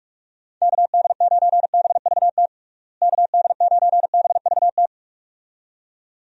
Morse code, heard as a keyed tone, about 46 wpm, 700 Hz. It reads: KD9BVT KD9BVT
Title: USA Call Signs - 6 Characters - Repeated 2x - 45wpm